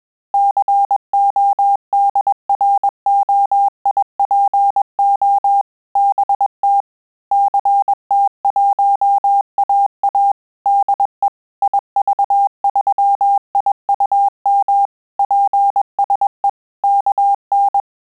Morse code, heard as a keyed tone, about 21 wpm, 800 Hz. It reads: CODROIPO BT CT1AA DE IV3IUM PSE KN
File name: QSO CW 100 CPM